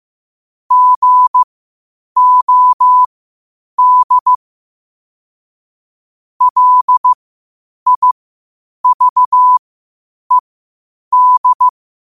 Morse code, heard as a keyed tone, about 15 wpm, spaced as for 8 wpm, 1000 Hz. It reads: GOD LIVED
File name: 8wpm